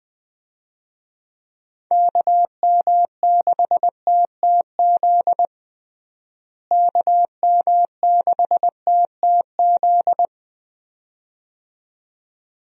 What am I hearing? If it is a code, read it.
Morse code, heard as a keyed tone, about 20 wpm, 700 Hz. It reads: KM6TTZ KM6TTZ
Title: USA Call Signs - 6 Characters - Repeated 2x - 20wpm